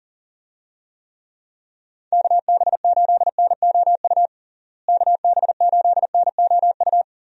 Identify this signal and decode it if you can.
Morse code, heard as a keyed tone, about 40 wpm, 700 Hz. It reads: KB8NOU KB8NOU
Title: USA Call Signs - 6 Characters - Repeated 2x - 40wpm